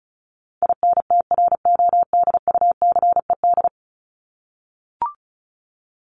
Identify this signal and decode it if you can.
Morse code, heard as a keyed tone, about 34 wpm, 700 Hz. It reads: INTRODUCED
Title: ED Suffix - Top 100 Words with Base Words 35wpm